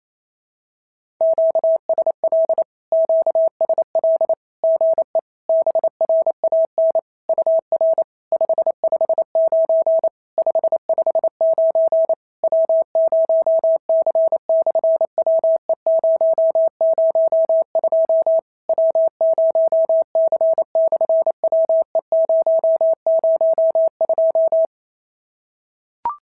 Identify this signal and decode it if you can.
Morse code, heard as a keyed tone, about 26 wpm, 650 Hz. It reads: QSL QSL GE BRAN UR 559 559 W0C/WE002 W0C/WE002
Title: SOTA QSO Practice 28wpm